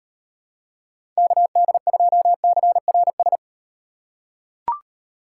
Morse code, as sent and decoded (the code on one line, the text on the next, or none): -.- -.. ..--- -.-. .-. ...
KD2CRS